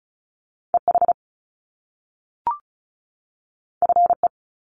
Morse code, heard as a keyed tone, about 36 wpm, 700 Hz. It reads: EH FE
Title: Top Two Letter Combinations in Words 35wpm